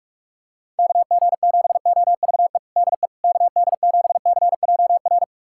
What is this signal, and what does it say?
Morse code, heard as a keyed tone, about 41 wpm, 700 Hz. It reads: KG7OVE DE KD7CJR
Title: USA Callsign DE Callsign 45wpm